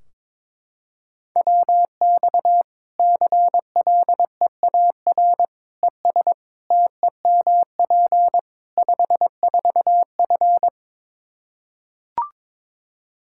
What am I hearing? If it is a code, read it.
Morse code, heard as a keyed tone, about 23 wpm, 700 Hz. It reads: WX CLEAR ES TEMP 54F
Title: Practice QSO - Exchange 2 - Info - 22wpm